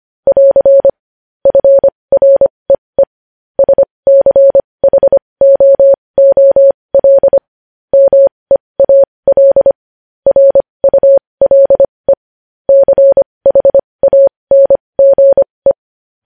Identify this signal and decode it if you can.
Morse code, heard as a keyed tone, about 25 wpm, spaced as for 20 wpm, 550 Hz. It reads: <AR> FREE SCHOOL MEAL RULE CHANGE
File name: podcast / News headlines-1x-25x20